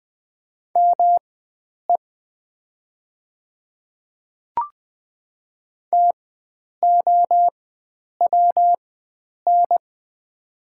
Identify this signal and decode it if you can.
Morse code, heard as a keyed tone, about 20 wpm, spaced as for 9 wpm, 700 Hz. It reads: ME TOWN
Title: 4x Character Spacing - Top 500 Words - 20wpm